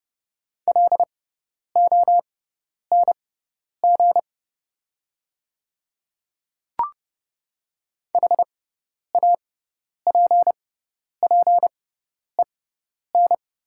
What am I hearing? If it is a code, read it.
Morse code, heard as a keyed tone, about 30 wpm, spaced as for 10 wpm, 690 Hz. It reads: LONG HAPPEN
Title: 6x Character Spacing - Top 500 Words - 30wpm